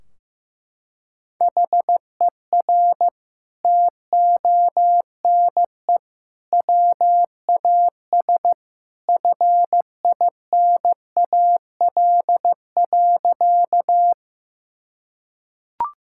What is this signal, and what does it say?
Morse code, heard as a keyed tone, about 15 wpm, 700 Hz. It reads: HER TONE WAS FINAL.
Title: Sentences from Top 1600 Words 15wpm